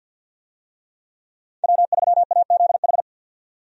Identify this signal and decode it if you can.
Morse code, heard as a keyed tone, about 46 wpm, 700 Hz. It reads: W3AZH
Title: USA Call Signs - 5 Characters - 50wpm